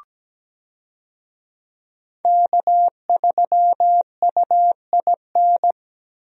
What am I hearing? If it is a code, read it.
Morse code, heard as a keyed tone, about 17 wpm, 700 Hz. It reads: K3UIN
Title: USA Call Signs - 5 Characters - Repeated 3x - 17wpm